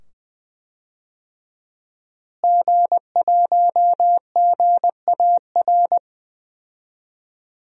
Morse code, heard as a keyed tone, about 20 wpm, 700 Hz. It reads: G1GAR
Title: DXCC-100 Call Signs - 5 Characters - Repeated 2x - 20wpm